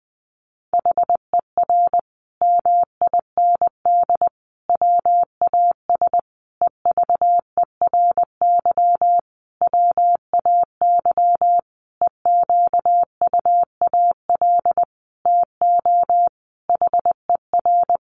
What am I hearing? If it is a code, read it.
Morse code, heard as a keyed tone, about 20 wpm, 700 Hz. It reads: HER MIND WAS EVERY WAY EQUAL TO HER
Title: Sentences from Top 1400 Words 20wpm